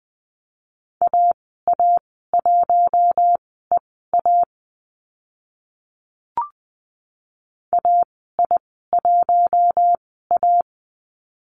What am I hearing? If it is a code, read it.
Morse code, heard as a keyed tone, about 20 wpm, spaced as for 15 wpm, 700 Hz. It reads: AA1EA AI1A
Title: 042 Lesson 8g - Number 1 - Call Signs 20wpm